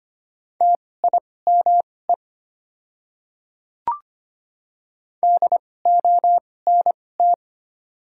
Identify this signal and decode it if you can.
Morse code, heard as a keyed tone, about 25 wpm, spaced as for 18 wpm, 700 Hz. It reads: TIME DONT
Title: 2x Character Spacing - Top 300 Words - 25wpm